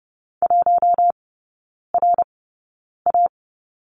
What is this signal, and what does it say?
Morse code, heard as a keyed tone, about 30 wpm, spaced as for 9 wpm, 700 Hz. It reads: 1RA
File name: Lesson 16e - Letter c - Three Characters 30wpm